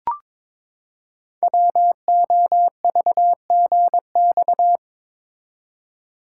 Morse code, heard as a keyed tone, about 22 wpm, 700 Hz. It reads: WOVGX
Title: Code Groups - 5 Letters - 22wpm